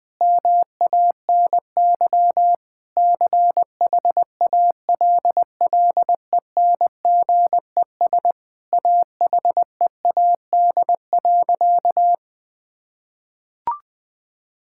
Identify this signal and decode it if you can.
Morse code, heard as a keyed tone, about 21 wpm, 700 Hz. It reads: MANY CHALLENGES AHEAD.
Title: Sentences with 5-Word Phrases - 20wpm